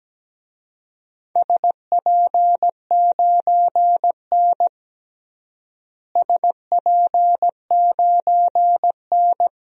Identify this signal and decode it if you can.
Morse code, heard as a keyed tone, about 17 wpm, 700 Hz. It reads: SP9N SP9N
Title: DXCC-100 Call Signs - 4 Characters - Repeated 2x - 17wpm